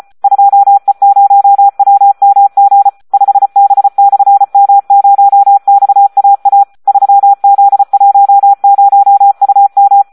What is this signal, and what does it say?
Morse code, heard as a keyed tone, about 34 wpm, 800 Hz. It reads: 2E0WMG 5B/M0XAA 3Z10UM
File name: Audio practica CW / SD 1 34